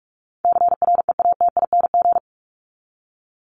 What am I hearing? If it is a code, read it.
Morse code, heard as a keyed tone, about 44 wpm, 700 Hz. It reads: CREATING
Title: ING Suffix - Top 100 Words with Base Words 45wpm